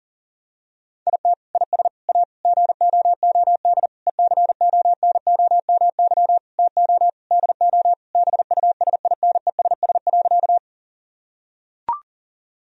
Morse code, heard as a keyed tone, about 40 wpm, 700 Hz. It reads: IT IS A GOOD ECONOMY TO DO BUSINESS.
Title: Sentences from Top 1000 Words - Review Part 3 - 40wpm